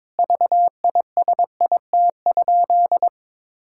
Morse code, heard as a keyed tone, about 21 wpm, 700 Hz. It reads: VISIT?
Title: Sentences from Top 1600 Words 22wpm